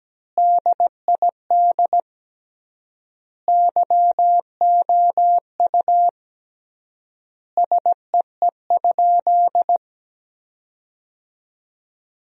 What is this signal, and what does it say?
Morse code, heard as a keyed tone, about 17 wpm, 700 Hz. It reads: DID YOU SEE?
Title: 3x Word-Spacing - Sentences from Top 100 Words - 17wpm